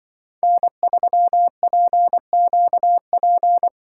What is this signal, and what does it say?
Morse code, heard as a keyed tone, about 24 wpm, 700 Hz. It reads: N3PQP
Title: Speed Racing - USA Call Signs 20wpm